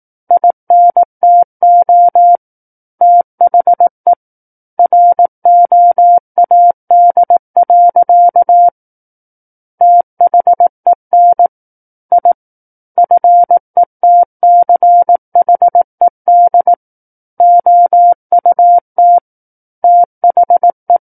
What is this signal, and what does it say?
Morse code, heard as a keyed tone, about 18 wpm, 700 Hz. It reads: INTO THE ROAD. THEN I FETCHED OUT THE
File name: 18 WPM